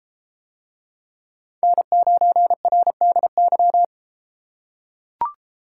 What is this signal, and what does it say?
Morse code, heard as a keyed tone, about 33 wpm, 700 Hz. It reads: N9RDY